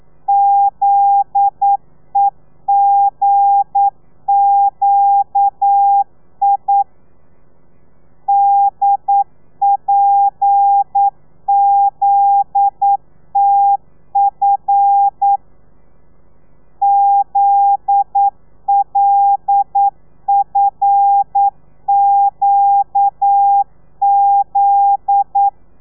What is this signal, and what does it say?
Morse code, heard as a keyed tone, about 9 wpm, 800 Hz. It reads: ZEGQI DPZTF ZLFQZ
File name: telegrafia / i2tfj / Lez9 40